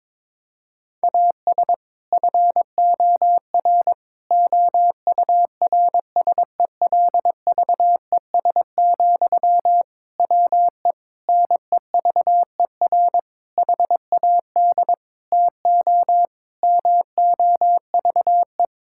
Morse code, heard as a keyed tone, about 22 wpm, 700 Hz. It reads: AS FOR OURSELVES, WE NEVER HAD TO MOVE
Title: Sentences from Top 1600 Words 22wpm